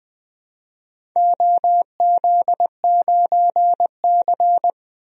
Morse code, heard as a keyed tone, about 20 wpm, 700 Hz. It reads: OZ9C
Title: DXCC-100 Call Signs - 4 Characters - Repeated 2x - 20wpm